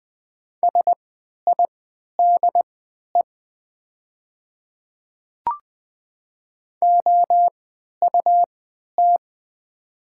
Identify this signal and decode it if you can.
Morse code, heard as a keyed tone, about 20 wpm, spaced as for 11 wpm, 700 Hz. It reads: SIDE OUT
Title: 3x Character Spacing - Top 100 Words - 20wpm